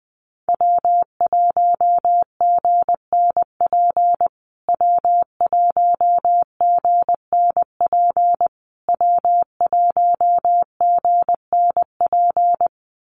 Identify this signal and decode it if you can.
Morse code, heard as a keyed tone, about 20 wpm, 700 Hz. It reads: W1GNP W1GNP W1GNP
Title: USA Call Signs Repeated 3x 20wpm